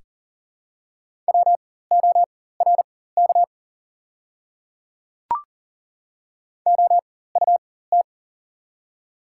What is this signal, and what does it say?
Morse code, heard as a keyed tone, about 40 wpm, spaced as for 19 wpm, 700 Hz. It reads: WORK OUT
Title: 4x Character Spacing - Top 300 Words - 40wpm